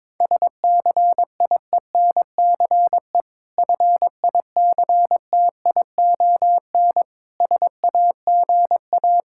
Morse code, heard as a keyed tone, about 22 wpm, 690 Hz. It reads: SCIENCE FICTION SAGA
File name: Difficult Words with Hints - 22wpm